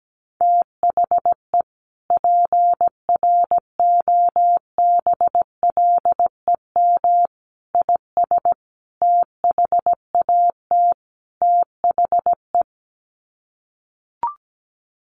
Morse code, heard as a keyed tone, about 17 wpm, 700 Hz. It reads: THE PROBLEM IS THAT THE
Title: Sets of 5 Words Encore2 17wpm